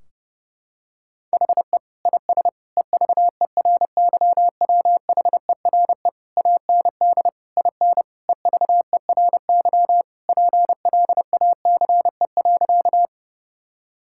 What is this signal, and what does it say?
Morse code, heard as a keyed tone, about 30 wpm, 700 Hz. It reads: HE IS EVERYWHERE AND IN EVERY PLACE.
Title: Sentences from Top 1600 Words 30wpm